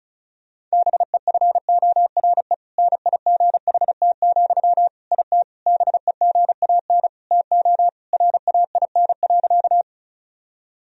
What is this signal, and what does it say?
Morse code, heard as a keyed tone, about 34 wpm, 700 Hz. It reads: BEFORE NIGHT, IT BEGAN TO RAIN.
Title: Sentences from Top 1000 Words - Review Part 1 - 35wpm